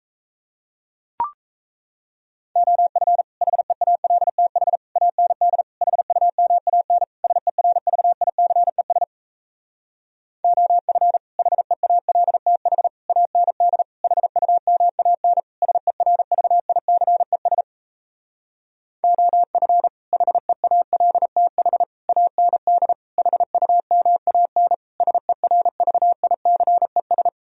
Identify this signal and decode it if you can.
Morse code, heard as a keyed tone, about 38 wpm, 700 Hz. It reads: OF HEALTH AND HUMAN SERVICES OF HEALTH AND HUMAN SERVICES OF HEALTH AND HUMAN SERVICES